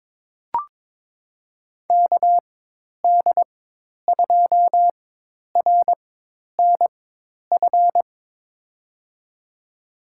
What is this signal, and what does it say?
Morse code, heard as a keyed tone, about 22 wpm, spaced as for 10 wpm, 700 Hz. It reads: KD2RNF
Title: USA Call Signs - 6 Characters - 4x Character Spacing - 22wpm